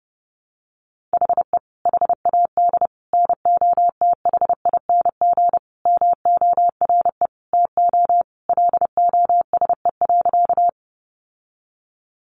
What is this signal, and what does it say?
Morse code, heard as a keyed tone, about 30 wpm, 700 Hz. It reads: HE HAD NOTHING MORE TO LOSE.
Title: Sentences from Top 1100 Words 30wpm